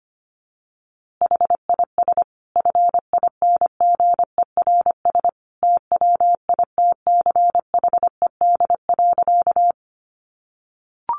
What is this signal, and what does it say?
Morse code, heard as a keyed tone, about 25 wpm, 700 Hz. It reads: HIS FINGERS TWITCHED.
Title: A Sound Of Thunder 25wpm